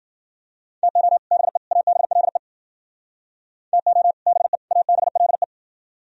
Morse code, heard as a keyed tone, about 53 wpm, 700 Hz. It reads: TO BE ABLE TO BE ABLE